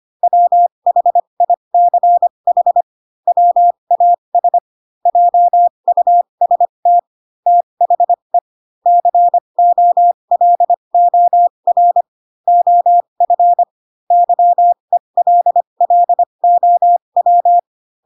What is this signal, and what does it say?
Morse code, heard as a keyed tone, about 25 wpm, spaced as for 22 wpm, 700 Hz. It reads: WHICH WAS JUST THE COLOR OF YELLOW
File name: Pinocchio / Pinocchio ch2 18 25 ik1hge